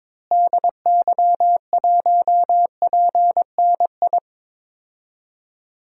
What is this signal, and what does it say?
Morse code, heard as a keyed tone, about 22 wpm, 700 Hz. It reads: DY1PNI
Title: DXCC-100 Call Signs - 6 Characters - Repeated 2x - 22wpm